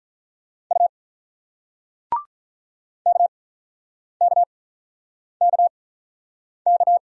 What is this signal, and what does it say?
Morse code, heard as a keyed tone, about 44 wpm, spaced as for 9 wpm, 700 Hz. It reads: U KKKK
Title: Speed-Racing Single Letter-Number 35wpm